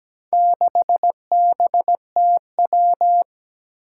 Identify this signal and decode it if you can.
Morse code, heard as a keyed tone, about 17 wpm, 700 Hz. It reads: 6BTW
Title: Four Challenging Letter-Number Combinations 17wpm